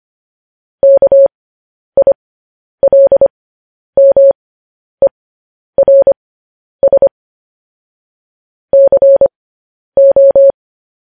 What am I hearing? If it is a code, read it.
Morse code, heard as a keyed tone, about 25 wpm, spaced as for 10 wpm, 550 Hz. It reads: KILMERS CO
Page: news headlines in morse code at 10 wpm